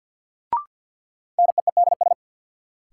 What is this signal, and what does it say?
Morse code, heard as a keyed tone, about 47 wpm, 700 Hz. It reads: NEEDS
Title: Sets of 3 Words - New Word Drill - 50wpm